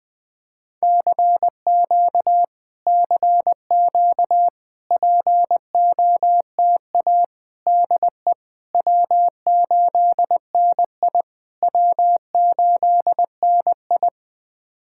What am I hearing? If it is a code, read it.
Morse code, heard as a keyed tone, about 20 wpm, 700 Hz. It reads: CQ CQ POTA DE W8NI W8NI
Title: POTA QSO Practice - 20wpm